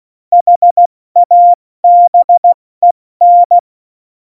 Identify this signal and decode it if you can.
Morse code, heard as a keyed tone, about 16 wpm, 700 Hz. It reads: HABEN
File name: Morsekurs / Witze / Witz 08. Januar 2019, 70 F80